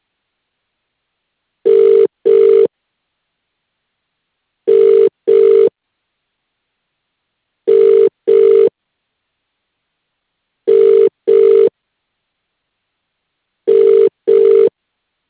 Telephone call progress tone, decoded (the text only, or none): ringback tone